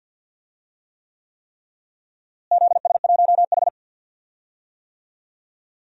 Morse code, heard as a keyed tone, about 51 wpm, 700 Hz. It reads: ZS1H
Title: DXCC-100 Call Signs - 4 Characters - Repeated 3x - 50wpm